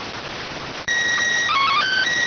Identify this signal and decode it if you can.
SSTV (slow-scan television), PD 290.